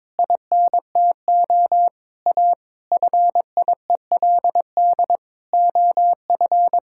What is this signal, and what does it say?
Morse code, heard as a keyed tone, about 22 wpm, 700 Hz. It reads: INTO A FIELD OF
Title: Pangram Challenge v2 - 22wpm